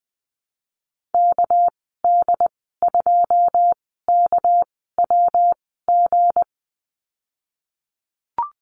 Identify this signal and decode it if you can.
Morse code, heard as a keyed tone, about 20 wpm, spaced as for 15 wpm, 700 Hz. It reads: KD2KWG
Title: 231 Lesson 37g - Letter x - Call Signs 20wpm